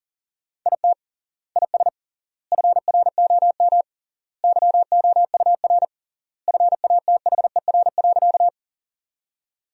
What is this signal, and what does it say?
Morse code, heard as a keyed tone, about 40 wpm, 700 Hz. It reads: IT IS FROM YOUR FATHER.
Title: 3x Word-Spacing - Sentences from Top 300 Words - 40wpm